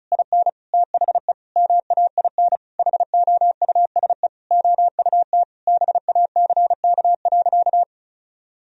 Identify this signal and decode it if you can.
Morse code, heard as a keyed tone, about 34 wpm, 700 Hz. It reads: IN THE MAIN HOUSE OUT BACK.
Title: Sentences from Top 1000 Words - Review Part 2 - 35wpm